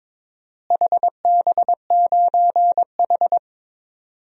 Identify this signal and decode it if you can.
Morse code, heard as a keyed tone, about 21 wpm, 700 Hz. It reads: HB9H
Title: DXCC-100 Call Signs - 4 Characters - Repeated 3x - 22wpm